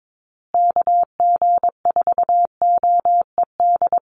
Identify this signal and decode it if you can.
Morse code, heard as a keyed tone, about 22 wpm, 700 Hz. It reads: KG4OED
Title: USA Call Signs Repeated 3x 22wpm